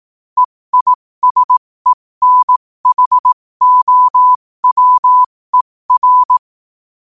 Morse code, heard as a keyed tone, about 18 wpm, spaced as for 15 wpm, 1000 Hz. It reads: EISENHOWER
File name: qotdfiles / 15wpm